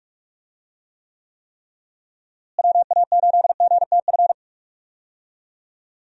Morse code, heard as a keyed tone, about 47 wpm, 700 Hz. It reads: WA8GTF
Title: USA Call Signs - 6 Characters - Repeated 3x - 45wpm